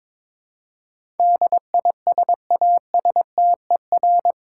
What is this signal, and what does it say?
Morse code, heard as a keyed tone, about 22 wpm, 700 Hz. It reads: DISASTER
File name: practice / 22wpm / Eight Letter Words - 22wpm